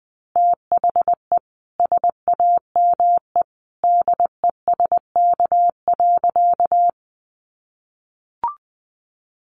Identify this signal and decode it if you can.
Morse code, heard as a keyed tone, about 20 wpm, 700 Hz. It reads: THE SAME DESK.
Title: A Sound Of Thunder 20wpm